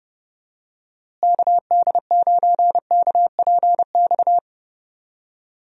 Morse code, heard as a keyed tone, about 30 wpm, 700 Hz. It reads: KD9KPX
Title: USA Call Signs - 6 Characters - Repeated 3x - 30wpm